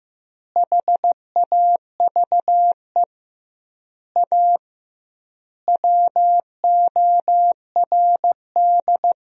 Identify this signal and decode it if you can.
Morse code, heard as a keyed tone, about 15 wpm, 700 Hz. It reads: HAVE A WORD
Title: 2x Word-Spacing - Sentences from Top 300 Words - 15wpm